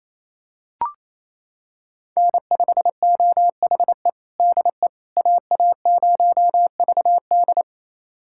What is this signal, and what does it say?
Morse code, heard as a keyed tone, about 28 wpm, 700 Hz. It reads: N5OHE DE AA0VD
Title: USA Callsign DE Callsign - 5 Characters - 28wpm